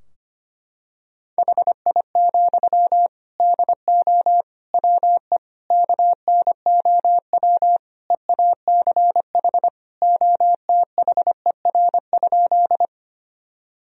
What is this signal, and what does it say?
Morse code, heard as a keyed tone, about 25 wpm, 700 Hz. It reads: HI, DO WE KNOW EACH OTHER?
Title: Sentences from Top 1000 Words - Review Part 3 - 25wpm